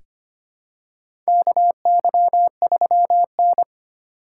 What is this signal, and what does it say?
Morse code, heard as a keyed tone, about 25 wpm, 700 Hz. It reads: KY3N
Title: USA Call Signs - 4 Characters - 6x Character Spacing - 25wpm